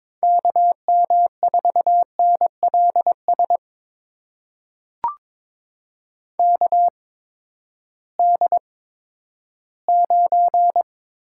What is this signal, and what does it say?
Morse code, heard as a keyed tone, about 21 wpm, spaced as for 12 wpm, 700 Hz. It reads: KM4NLS KD9
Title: USA Call Signs - 6 Characters - 8x Character Spacing - 22wpm